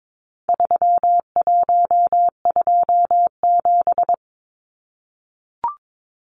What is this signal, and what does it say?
Morse code, heard as a keyed tone, about 22 wpm, 700 Hz. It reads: 3127
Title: Numbers - 4-Digit 22wpm